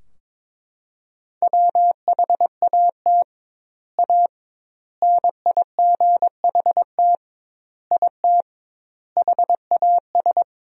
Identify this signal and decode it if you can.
Morse code, heard as a keyed tone, about 22 wpm, 700 Hz. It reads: WHAT A NIGHT IT HAS
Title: 2x Word-Spacing - Sentences from Top 300 Words - 22wpm